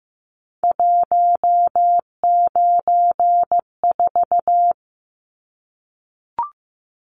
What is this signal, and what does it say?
Morse code, heard as a keyed tone, about 15 wpm, 700 Hz. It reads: 194
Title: Numbers - 3-Digit 15wpm